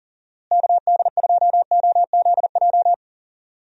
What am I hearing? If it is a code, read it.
Morse code, heard as a keyed tone, about 40 wpm, 700 Hz. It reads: KD2OZJ
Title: USA Call Signs - 6 Characters - Repeated 2x - 40wpm